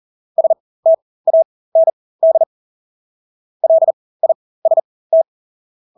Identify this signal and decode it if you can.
Morse code, heard as a keyed tone, about 40 wpm, spaced as for 20 wpm, 650 Hz. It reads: STAND LIST
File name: Engelske ord 40-20 wpm